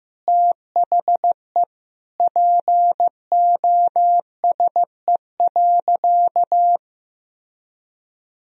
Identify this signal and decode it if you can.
Morse code, heard as a keyed tone, about 15 wpm, 700 Hz. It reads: THE POSE.
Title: Sentences from Top 1000 Words - Review Part 2 - 15wpm